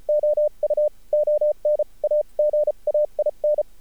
Morse code, heard as a keyed tone, about 36 wpm, spaced as for 30 wpm, 600 Hz. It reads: OU ON AGAIN
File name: archived / CRSnet